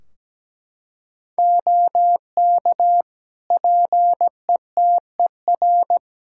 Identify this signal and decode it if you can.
Morse code, heard as a keyed tone, about 17 wpm, 700 Hz. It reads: OK PETER
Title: K1USN SST QSO Practice 17wpm